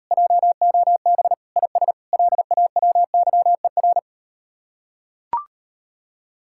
Morse code, heard as a keyed tone, about 38 wpm, 700 Hz. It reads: JOB IS LAWYER